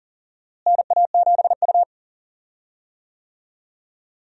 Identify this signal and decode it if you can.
Morse code, heard as a keyed tone, about 40 wpm, 700 Hz. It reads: NA7U
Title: USA Call Signs - 4 Characters - Repeated 2x - 40wpm